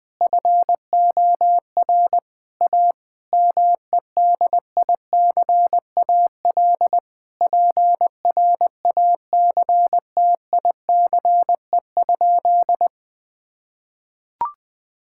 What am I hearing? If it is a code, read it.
Morse code, heard as a keyed tone, about 20 wpm, 700 Hz. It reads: FOR A MEDICAL PRACTICE?
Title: Sentences from Top 1000 Words - Review Part 5 - 20wpm